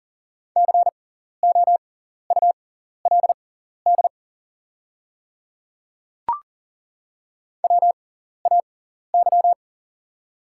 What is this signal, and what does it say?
Morse code, heard as a keyed tone, about 40 wpm, spaced as for 14 wpm, 700 Hz. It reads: COULD WAY
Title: 6x Character Spacing - Top 200 Words - 40wpm